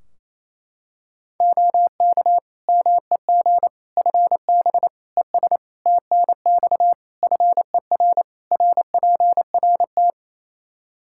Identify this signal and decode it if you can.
Morse code, heard as a keyed tone, about 28 wpm, 700 Hz. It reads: OK MEG FB ES TNX FER RPRT
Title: Practice QSO - Exchange 2 - Info - 28wpm